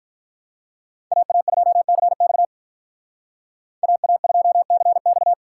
Morse code, heard as a keyed tone, about 42 wpm, 700 Hz. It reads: AA2CX AA2CX
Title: Speed Racing - USA 2x2 Call Signs 35wpm